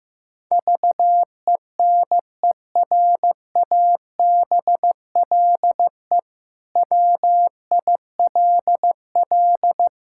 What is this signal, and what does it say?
Morse code, heard as a keyed tone, about 15 wpm, 700 Hz. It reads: VENERABLE WILL
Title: Pangram Challenge v2 - 15wpm